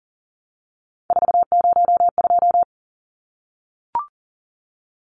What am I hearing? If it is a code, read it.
Morse code, heard as a keyed tone, about 40 wpm, 700 Hz. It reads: 402
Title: Numbers - 3-Digit 40wpm